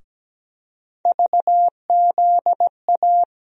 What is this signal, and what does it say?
Morse code, heard as a keyed tone, about 17 wpm, 700 Hz. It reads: VZA
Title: USA Call Sign Postfixes - 3 Letters 17wpm